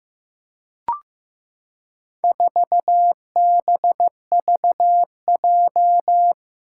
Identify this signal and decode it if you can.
Morse code, heard as a keyed tone, about 15 wpm, 700 Hz. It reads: 4BVJ
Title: Four Challenging Letter-Number Combinations 15wpm